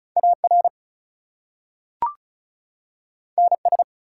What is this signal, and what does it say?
Morse code, heard as a keyed tone, about 36 wpm, 700 Hz. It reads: AR NS
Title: US States and Canadian Provinces 35wpm